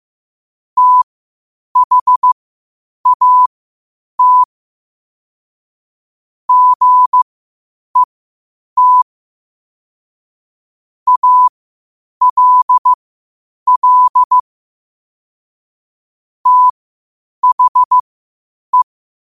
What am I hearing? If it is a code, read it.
Morse code, heard as a keyed tone, about 15 wpm, spaced as for 8 wpm, 1000 Hz. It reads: THAT GET ALL THE
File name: qotdfiles / 8wpm